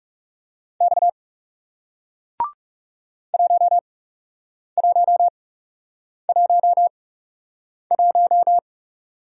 Morse code, heard as a keyed tone, about 39 wpm, spaced as for 8 wpm, 700 Hz. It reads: X 1111